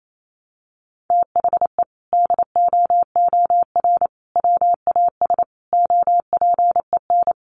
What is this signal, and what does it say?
Morse code, heard as a keyed tone, about 28 wpm, 700 Hz. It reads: THE DOOR WAS OPEN
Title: Sentences from Top 1300 Words 28wpm